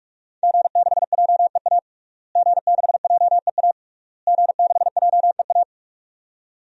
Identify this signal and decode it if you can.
Morse code, heard as a keyed tone, about 42 wpm, 700 Hz. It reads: G6JEA G6JEA G6JEA
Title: DXCC-100 Call Signs - 5 Characters - Repeated 3x - 45wpm